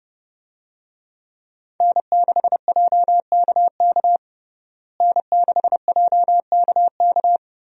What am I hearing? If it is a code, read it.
Morse code, heard as a keyed tone, about 32 wpm, 700 Hz. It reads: N6JKK N6JKK
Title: USA Call Signs - 5 Characters - Repeated 3x - 30wpm